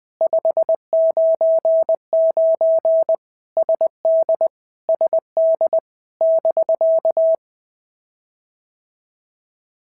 Morse code, heard as a keyed tone, about 20 wpm, 650 Hz. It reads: 599 SD SD <BK>
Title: POTA QSO Practice - 20wpm